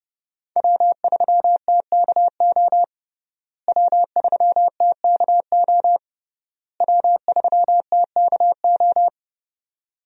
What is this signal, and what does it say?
Morse code, heard as a keyed tone, about 30 wpm, 700 Hz. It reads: W3TKO W3TKO W3TKO
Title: USA Call Signs - 5 Characters - Repeated 3x - 30wpm